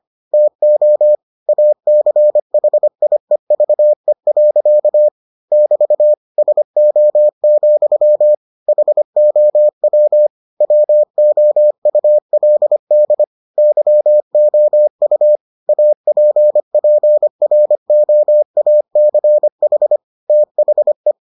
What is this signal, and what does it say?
Morse code, heard as a keyed tone, about 25 wpm, 600 Hz. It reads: TO ACHIEVE. <BT> SO, HOW WOULD YOU APPROACH THE